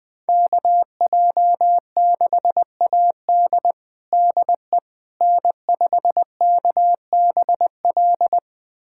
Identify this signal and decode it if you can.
Morse code, heard as a keyed tone, about 21 wpm, 700 Hz. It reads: KJ6AD DE N5KBL
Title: USA Callsign DE Callsign - 5 Characters - 20wpm